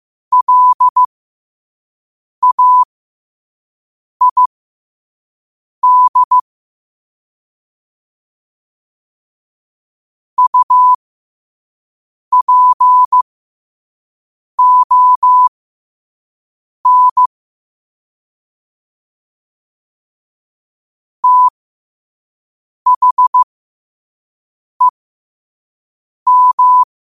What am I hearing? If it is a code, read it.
Morse code, heard as a keyed tone, about 15 wpm, spaced as for 5 wpm, 1000 Hz. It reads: LAID UPON THEM
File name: qotdfiles / 5wpm